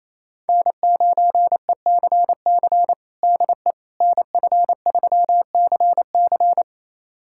Morse code, heard as a keyed tone, about 28 wpm, 700 Hz. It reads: N9ECC DE NF3CC
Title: USA Callsign DE Callsign 28wpm